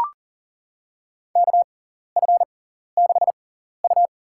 Morse code, heard as a keyed tone, about 40 wpm, spaced as for 14 wpm, 700 Hz. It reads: KF6U
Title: USA Call Signs - 6x Character Spacing - 40wpm